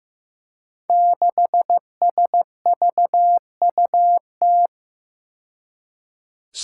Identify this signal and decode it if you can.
Morse code, heard as a keyed tone, about 15 wpm, 700 Hz. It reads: 6SVUT
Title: Five Challenging Letter-Number Combinations 15wpm